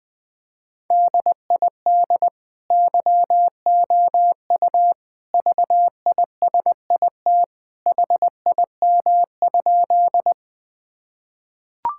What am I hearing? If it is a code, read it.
Morse code, heard as a keyed tone, about 20 wpm, 700 Hz. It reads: DID YOU VISIT HIM?
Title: Sentences from Top 1000 Words - Review Part 6 - 20wpm